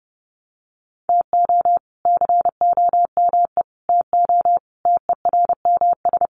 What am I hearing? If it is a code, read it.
Morse code, heard as a keyed tone, about 30 wpm, 700 Hz. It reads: TO COME TO TERMS
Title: Sets of 5 Words Encore 30wpm